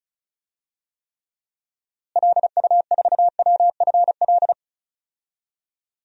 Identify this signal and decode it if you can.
Morse code, heard as a keyed tone, about 36 wpm, 700 Hz. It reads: LU4WFL
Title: DXCC-100 Call Signs - 6 Characters - Repeated 2x - 35wpm